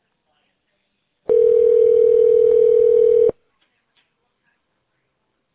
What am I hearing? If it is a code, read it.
Ringback tone (North America), 440 + 480 Hz.